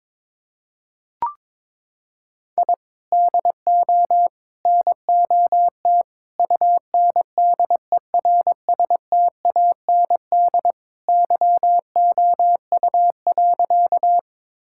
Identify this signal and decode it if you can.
Morse code, heard as a keyed tone, about 22 wpm, 700 Hz. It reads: I DO NOT UNDERSTAND YOU.
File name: practice / 22wpm / Sentences with 4-Word Phrases - 22wpm